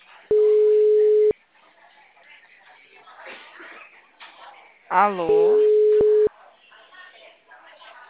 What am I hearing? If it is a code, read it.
Ringback tone (Europe, 425 Hz), with speech.